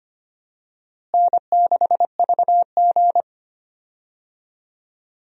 Morse code, heard as a keyed tone, about 25 wpm, 700 Hz. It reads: N6VG
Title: USA Call Signs - 4 Characters - Repeated 3x - 25wpm